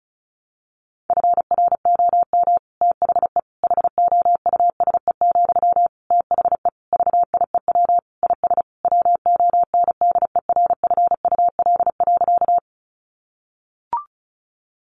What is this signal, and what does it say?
Morse code, heard as a keyed tone, about 34 wpm, 700 Hz. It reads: FROM THE HOUSE, THE VIEW IS WONDERFUL.
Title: Sentences from Top 1100 Words 35wpm